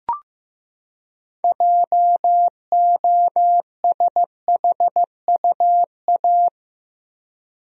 Morse code, heard as a keyed tone, about 15 wpm, 700 Hz. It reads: JOSHUA
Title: USA Common Names - 15wpm